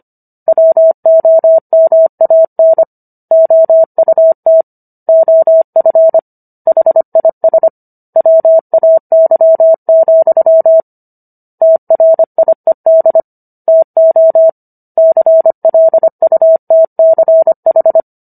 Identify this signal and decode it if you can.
Morse code, heard as a keyed tone, about 25 wpm, 650 Hz. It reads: WOMAN OUT OF HIS WAY, TRIED TO CLUTCH